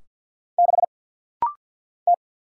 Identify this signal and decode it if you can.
Morse code, heard as a keyed tone, about 50 wpm, 700 Hz. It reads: 6 T
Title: Single Letter-Number Alt - Rapid-Fire 50wpm